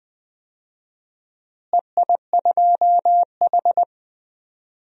Morse code, heard as a keyed tone, about 20 wpm, 700 Hz. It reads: EI2H
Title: DXCC-100 Call Signs - 4 Characters - Repeated 2x - 20wpm